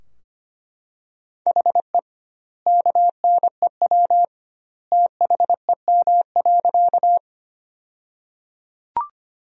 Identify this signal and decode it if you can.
Morse code, heard as a keyed tone, about 25 wpm, 700 Hz. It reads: HE KNEW THEM.
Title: 2x Word-Spacing - Sentences from Top 400 Words - 25wpm